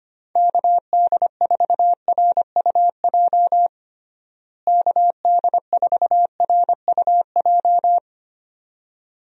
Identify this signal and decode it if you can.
Morse code, heard as a keyed tone, about 25 wpm, 700 Hz. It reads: KD4RUJ KD4RUJ
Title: USA Call Signs - 6 Characters - Repeated 2x - 25wpm